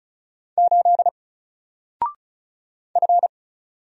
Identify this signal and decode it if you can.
Morse code, heard as a keyed tone, about 35 wpm, 700 Hz. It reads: 8 F
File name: Single-Letter-Number-35wpm